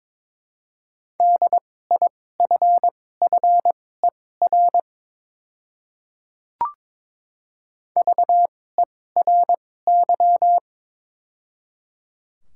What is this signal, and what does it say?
Morse code, heard as a keyed tone, about 21 wpm, spaced as for 16 wpm, 700 Hz. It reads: DIFFER VERY
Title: 2x Character Spacing - Top 200 Words - 22wpm